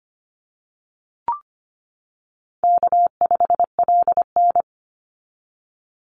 Morse code, heard as a keyed tone, about 25 wpm, 700 Hz. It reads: K5LN
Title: Call Signs - 25wpm